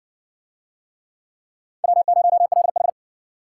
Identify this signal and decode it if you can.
Morse code, heard as a keyed tone, about 58 wpm, 700 Hz. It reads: W0LH